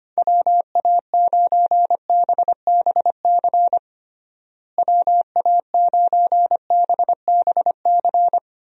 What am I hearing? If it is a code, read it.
Morse code, heard as a keyed tone, about 25 wpm, 700 Hz. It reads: WA9BBC WA9BBC
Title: USA Call Signs - 6 Characters - Repeated 2x - 25wpm